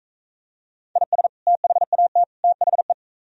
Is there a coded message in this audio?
Yes: Morse code from 1.0 to 2.9 s, heard as a keyed tone, about 44 wpm, 700 Hz. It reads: IS THAT THE